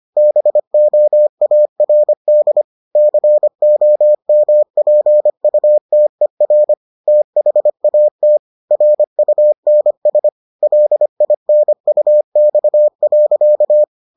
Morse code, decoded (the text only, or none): BOARD COMPUTER THAT RUNS LINUX.